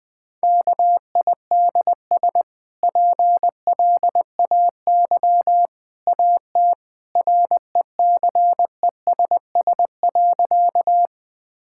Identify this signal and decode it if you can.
Morse code, heard as a keyed tone, about 20 wpm, 700 Hz. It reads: KIDS PLAY AT RECESS.
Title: Easy Sentences with Hints - 20wpm